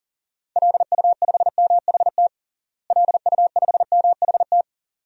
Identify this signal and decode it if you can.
Morse code, heard as a keyed tone, about 37 wpm, 700 Hz. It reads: LU5MHT LU5MHT
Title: DXCC-100 Call Signs - 6 Characters - Repeated 2x - 40wpm